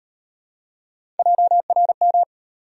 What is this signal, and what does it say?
Morse code, heard as a keyed tone, about 38 wpm, 700 Hz. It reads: JRM